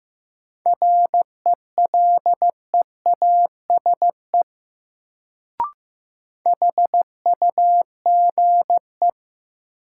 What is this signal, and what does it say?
Morse code, heard as a keyed tone, about 15 wpm, 700 Hz. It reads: RELEASE HUGE
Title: Sentences from Top 1100 Words - New Word Drill - 15wpm